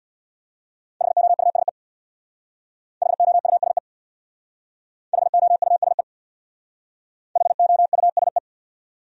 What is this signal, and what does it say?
Morse code, heard as a keyed tone, about 67 wpm, 700 Hz. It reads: HOUSE HOUSE HOUSE HOUSE